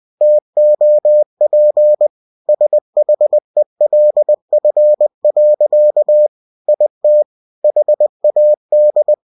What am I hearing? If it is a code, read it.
Morse code, heard as a keyed tone, about 20 wpm, 600 Hz. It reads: TOP SHELF. IT HAD
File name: cw-books / winnie-the-pooh / winnie-20wpm-page-0037